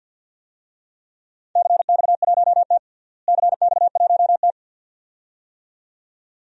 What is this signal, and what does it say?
Morse code, heard as a keyed tone, about 47 wpm, 700 Hz. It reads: CX1T CX1T
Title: DXCC-100 Call Signs - 4 Characters - Repeated 2x - 50wpm